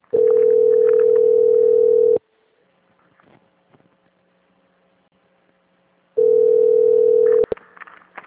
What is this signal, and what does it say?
Ringback tone (North America), 440 + 480 Hz.